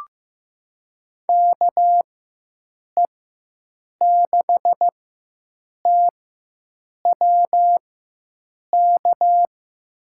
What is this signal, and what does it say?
Morse code, heard as a keyed tone, about 15 wpm, spaced as for 7 wpm, 700 Hz. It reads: KE6TWK
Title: USA Call Signs - 6 Characters - 4x Character Spacing - 15wpm